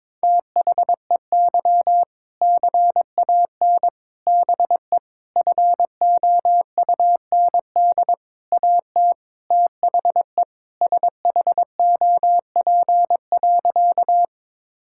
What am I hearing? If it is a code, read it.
Morse code, heard as a keyed tone, about 22 wpm, 700 Hz. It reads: THEY CAN BE FOUND AT THE SHOP.
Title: Sentences from Top 1600 Words 22wpm